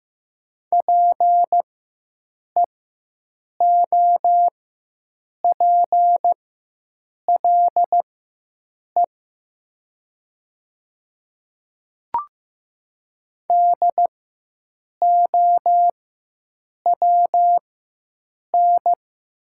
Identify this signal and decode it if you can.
Morse code, heard as a keyed tone, about 15 wpm, spaced as for 7 wpm, 700 Hz. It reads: PEOPLE DOWN
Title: 4x Character Spacing - Top 100 Words - 15wpm